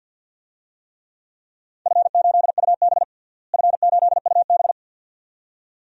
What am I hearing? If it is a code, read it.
Morse code, heard as a keyed tone, about 50 wpm, 700 Hz. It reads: F8UB F8UB
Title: DXCC-100 Call Signs - 4 Characters - Repeated 2x - 50wpm